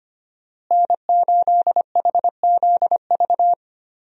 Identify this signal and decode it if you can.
Morse code, heard as a keyed tone, about 25 wpm, 700 Hz. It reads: N8HZV
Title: USA Call Signs - 5 Characters - Repeated 3x - 25wpm